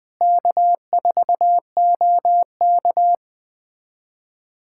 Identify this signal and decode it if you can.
Morse code, heard as a keyed tone, about 21 wpm, 700 Hz. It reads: K4OK
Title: USA Callsign DE Callsign - 4 Characters - 20wpm